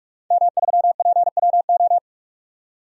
Morse code, heard as a keyed tone, about 46 wpm, 700 Hz. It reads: M3PWO
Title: DXCC-100 Call Signs - 5 Characters - 45wpm